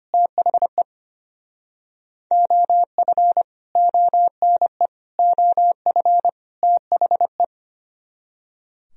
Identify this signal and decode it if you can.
Morse code, heard as a keyed tone, about 26 wpm, 700 Hz. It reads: THE OF ONE OF THE
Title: Speed-Racing - Sets of 4 Words 25wpm